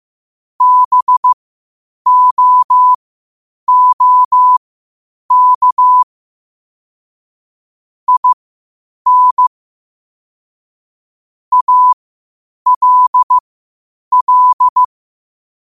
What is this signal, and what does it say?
Morse code, heard as a keyed tone, about 15 wpm, spaced as for 8 wpm, 1000 Hz. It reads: BOOK IN ALL
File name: qotdfiles / 8wpm